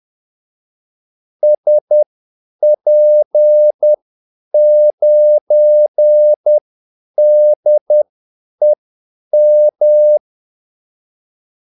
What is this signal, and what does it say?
Morse code, heard as a keyed tone, about 10 wpm, spaced as for 8 wpm, 600 Hz. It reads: SP9DEM
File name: Callsigns-10-8wpm